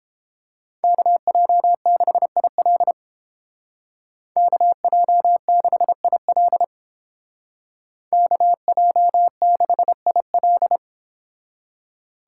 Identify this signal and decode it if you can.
Morse code, heard as a keyed tone, about 29 wpm, 700 Hz. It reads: KJ6IL KJ6IL KJ6IL